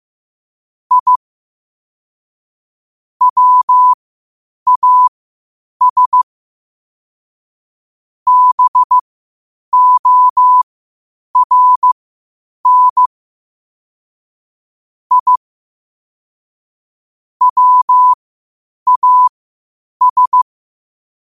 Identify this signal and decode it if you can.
Morse code, heard as a keyed tone, about 15 wpm, spaced as for 8 wpm, 1000 Hz. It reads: I WAS BORN I WAS
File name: qotdfiles / 8wpm